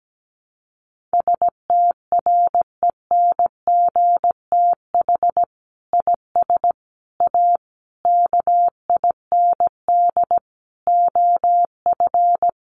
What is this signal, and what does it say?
Morse code, heard as a keyed tone, about 17 wpm, 700 Hz. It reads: STRENGTH IS A KIND OF
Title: Sentences from Top 1300 Words 17wpm